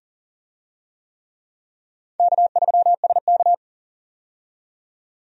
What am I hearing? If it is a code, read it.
Morse code, heard as a keyed tone, about 40 wpm, 700 Hz. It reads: K3SK
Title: USA Call Signs - 4 Characters - 40wpm